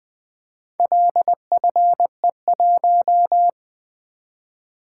Morse code, heard as a keyed tone, about 23 wpm, 700 Hz. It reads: LFE1
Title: Speed-Racing - Four Letter-Number Combinations - 20wpm